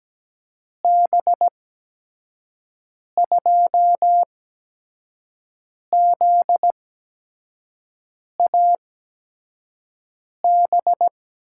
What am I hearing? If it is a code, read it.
Morse code, heard as a keyed tone, about 17 wpm, spaced as for 5 wpm, 700 Hz. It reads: B2ZAB
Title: USA Call Signs - 6 Characters - 8x Character Spacing - 17wpm